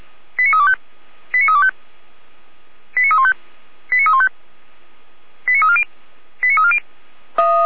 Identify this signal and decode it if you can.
Five-tone selective call (ZVEI), heard as quick runs of five tones, tones of about 70 ms. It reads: 89325 89325 89315 89315 89340 89340
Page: /455/SDR/ZVEI/89015/